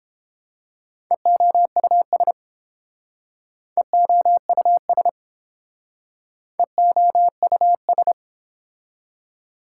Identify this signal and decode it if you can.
Morse code, heard as a keyed tone, about 30 wpm, 700 Hz. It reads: EOUS EOUS EOUS